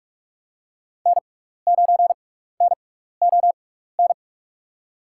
Morse code, heard as a keyed tone, about 44 wpm, spaced as for 15 wpm, 690 Hz. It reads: N9NON
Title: USA Call Signs - 5 Characters - 6x Character Spacing - 45wpm